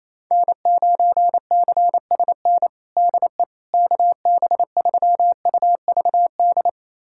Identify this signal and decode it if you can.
Morse code, heard as a keyed tone, about 27 wpm, 700 Hz. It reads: N9CSN DE KB3UVD
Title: USA Callsign DE Callsign 28wpm